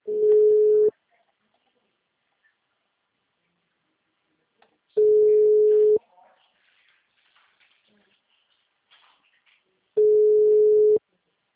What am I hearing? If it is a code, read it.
Ringback tone (Europe, 425 Hz).